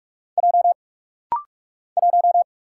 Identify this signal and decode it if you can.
Morse code, heard as a keyed tone, about 44 wpm, 700 Hz. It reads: J 1
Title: Single Letter-Number - Rapid-Fire 45wpm